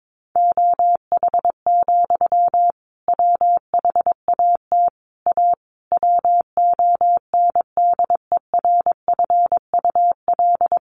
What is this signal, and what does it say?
Morse code, heard as a keyed tone, about 22 wpm, 700 Hz. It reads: OH, WHAT A WONDERFUL
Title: Sentences from Top 1100 Words 22wpm